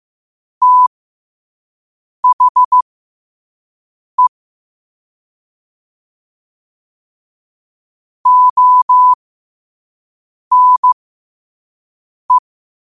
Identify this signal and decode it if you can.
Morse code, heard as a keyed tone, about 15 wpm, spaced as for 5 wpm, 1000 Hz. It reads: THE ONE
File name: qotdfiles / 5wpm